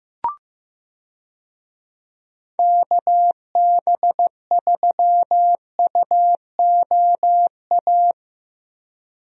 Morse code, heard as a keyed tone, about 15 wpm, 700 Hz. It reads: KB3UOA
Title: USA Call Signs - 6 Characters - Repeated 2x - 15wpm